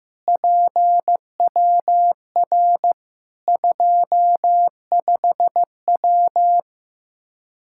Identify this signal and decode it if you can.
Morse code, heard as a keyed tone, about 15 wpm, 700 Hz. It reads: PWR 25W
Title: Practice QSO - Exchange 2 - Info - 15wpm